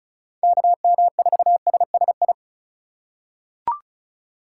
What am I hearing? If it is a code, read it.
Morse code, heard as a keyed tone, about 34 wpm, 700 Hz. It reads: KM4SSI
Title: USA Call Signs - 6 Characters - 2x Character Spacing - 35wpm